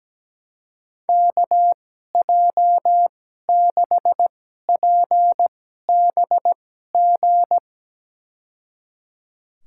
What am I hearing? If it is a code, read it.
Morse code, heard as a keyed tone, about 17 wpm, spaced as for 12 wpm, 700 Hz. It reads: KJ6PBG
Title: USA Call Signs - 6 Characters - 2x Character Spacing - 17wpm